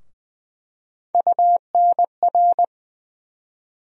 Morse code, heard as a keyed tone, about 20 wpm, 700 Hz. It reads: UNR
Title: USA Call Sign Postfixes - 3 Letters 20wpm